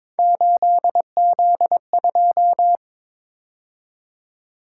Morse code, heard as a keyed tone, about 26 wpm, 700 Hz. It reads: 8Z2
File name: Speed-Racing - Three Letter-Number Combinations - 22wpm